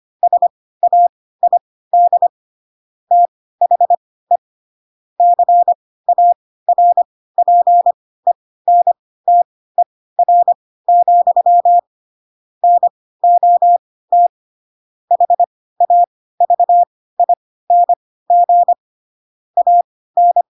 Morse code, heard as a keyed tone, about 25 wpm, spaced as for 16 wpm, 700 Hz. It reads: SAID THE CARPENTER, NOT HAVING AN